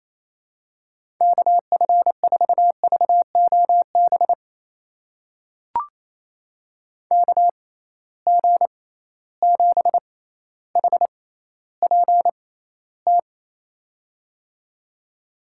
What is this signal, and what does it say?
Morse code, heard as a keyed tone, about 28 wpm, spaced as for 14 wpm, 700 Hz. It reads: KF4VOB KG7HPT
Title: USA Call Signs - 6 Characters - 6x Character Spacing - 28wpm